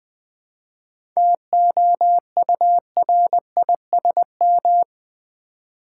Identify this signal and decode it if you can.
Morse code, heard as a keyed tone, about 20 wpm, 700 Hz. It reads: TOURISM
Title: 7 Letter Words - First 3 Letters Given - 20wpm